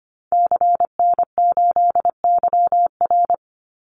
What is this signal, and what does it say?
Morse code, heard as a keyed tone, about 25 wpm, 700 Hz. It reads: CN8YR
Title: Call Signs Repeated 3x - 25wpm